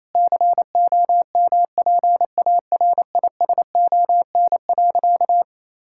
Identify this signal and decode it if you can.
Morse code, heard as a keyed tone, about 28 wpm, 700 Hz. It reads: COMPARISON.
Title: Sentences from Top 1600 Words 28wpm